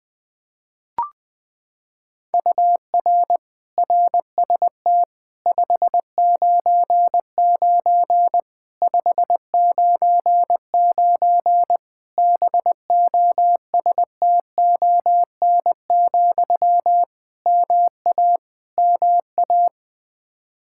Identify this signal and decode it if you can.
Morse code, heard as a keyed tone, about 21 wpm, 700 Hz. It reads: UR RST 599 599 BOSTON, MA MA
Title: Practice QSO - Exchange 1 - Beginning - 20wpm